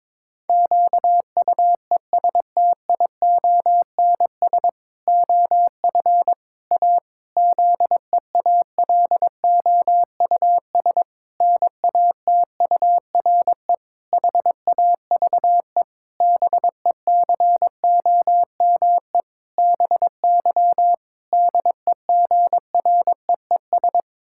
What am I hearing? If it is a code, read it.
Morse code, heard as a keyed tone, about 22 wpm, 700 Hz. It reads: QUESTIONS OF A ZEALOUS NATURE HAVE BECOME BY DEGREES
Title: Pangram Challenge v2 - 22wpm